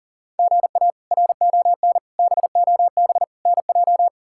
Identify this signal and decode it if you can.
Morse code, heard as a keyed tone, about 40 wpm, 700 Hz. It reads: GA RON BOB NJ
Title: K1USN SST QSO Practice - 40wpm